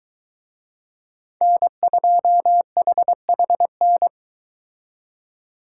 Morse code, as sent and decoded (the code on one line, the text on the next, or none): -. ..--- .... .... -.
N2HHN